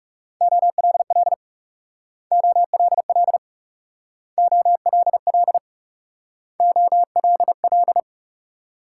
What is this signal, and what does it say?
Morse code, heard as a keyed tone, about 38 wpm, 700 Hz. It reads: OLL OLL OLL OLL